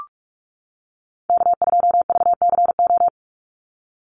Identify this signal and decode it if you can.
Morse code, heard as a keyed tone, about 44 wpm, 700 Hz. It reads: K2VCO
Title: Call Signs - 45wpm